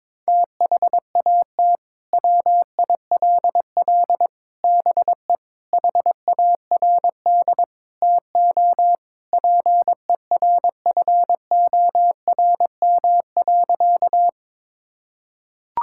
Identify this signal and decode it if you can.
Morse code, heard as a keyed tone, about 22 wpm, 700 Hz. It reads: THAT WILL BE HARD TO PERFORM.
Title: Sentences from Top 1000 Words - Review Part 5 - 22wpm